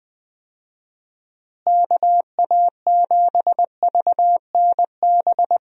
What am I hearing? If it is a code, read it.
Morse code, heard as a keyed tone, about 20 wpm, 700 Hz. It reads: KA7VNB
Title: USA Call Signs - 6 Characters - Repeated 3x - 20wpm